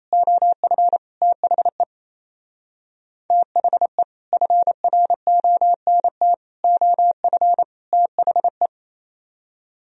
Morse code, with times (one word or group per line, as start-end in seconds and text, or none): 0.1-1.0 OF
1.2-1.8 THE
3.3-4.0 THE
4.3-6.3 FRONT
6.6-7.6 OF
7.9-8.7 THE